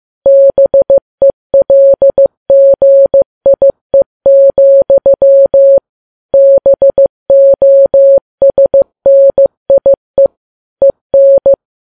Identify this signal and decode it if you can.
Morse code, heard as a keyed tone, about 15 wpm, 550 Hz. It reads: BELGIE, BOSNIE EN
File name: Files Web tls / Morse on9fv / morse15kps